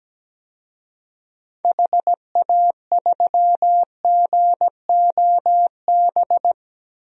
Morse code, heard as a keyed tone, about 17 wpm, 700 Hz. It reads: HA3GOB
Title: DXCC-100 Call Signs - 6 Characters - Repeated 3x - 17wpm